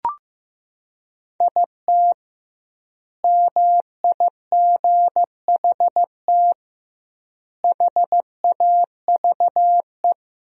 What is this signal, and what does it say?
Morse code, heard as a keyed tone, about 16 wpm, 700 Hz. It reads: IT MIGHT HAVE
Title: 2x Word-Spacing - Sentences from Top 500 Words - 15wpm